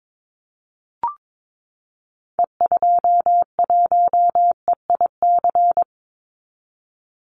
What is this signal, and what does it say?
Morse code, heard as a keyed tone, about 22 wpm, 700 Hz. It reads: E21EIC
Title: Call Signs Repeated 3x - 22wpm